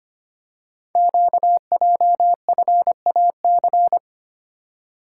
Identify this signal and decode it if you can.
Morse code, heard as a keyed tone, about 25 wpm, 700 Hz. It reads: QJFAC
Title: Code Groups - 5 Letters - 25wpm